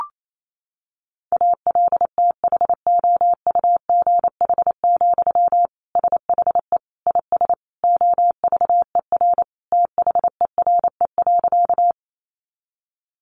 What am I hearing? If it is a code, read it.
Morse code, heard as a keyed tone, about 28 wpm, 700 Hz. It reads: ALTHOUGH, SHE IS OVER THERE.
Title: Sentences from Top 600 Words 28wpm